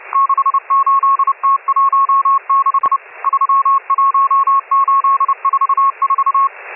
Morse code, heard as a keyed tone, about 30 wpm, 1100 Hz, with radio noise. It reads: 68T16 31844
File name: M12- -1010utc